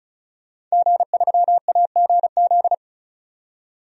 Morse code, heard as a keyed tone, about 36 wpm, 700 Hz. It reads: G3AGZ
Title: DXCC-100 Call Signs - 5 Characters - 35wpm